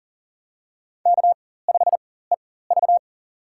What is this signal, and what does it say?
Morse code, heard as a keyed tone, about 40 wpm, spaced as for 19 wpm, 700 Hz. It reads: K5EV
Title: USA Call Signs - 4 Characters - 4x Character Spacing - 40wpm